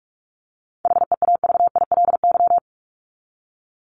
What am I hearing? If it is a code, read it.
Morse code, heard as a keyed tone, about 44 wpm, 700 Hz. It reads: HEAVILY
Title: LY Suffix - Top 100 Words with Base Words 45wpm